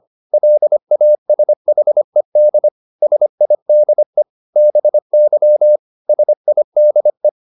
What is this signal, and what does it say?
Morse code, heard as a keyed tone, about 25 wpm, 600 Hz. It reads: LASHED SIDE BY SIDE